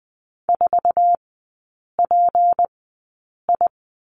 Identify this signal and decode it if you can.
Morse code, heard as a keyed tone, about 20 wpm, spaced as for 8 wpm, 700 Hz. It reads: 4PI
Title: 152 Lesson 25e - Letter p - Three Characters 20wpm